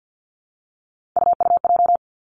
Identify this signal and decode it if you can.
Morse code, heard as a keyed tone, about 50 wpm, 700 Hz. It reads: UUJ
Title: Three Challenging Letter-Number Combinations 50wpm